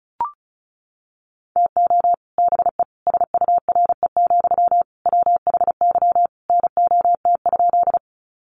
Morse code, heard as a keyed tone, about 34 wpm, 700 Hz. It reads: TO BE SURE, WHY NOT?
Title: Sentences from Top 600 Words 35wpm